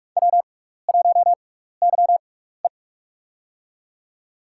Morse code, heard as a keyed tone, about 43 wpm, spaced as for 15 wpm, 700 Hz. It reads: W1YE
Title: USA Call Signs - 4 Characters - 6x Character Spacing - 45wpm